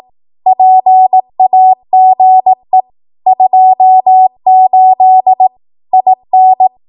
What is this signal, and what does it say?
Morse code, heard as a keyed tone, about 18 wpm, 750 Hz. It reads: PAGE 28 IN
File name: Archive / 18 WPM / 18WPM